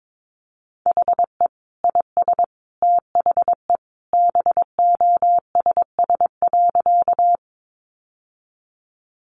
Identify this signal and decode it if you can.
Morse code, heard as a keyed tone, about 22 wpm, 700 Hz. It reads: HE IS THE BOSS.
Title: Sentences from Top 700 Words 22wpm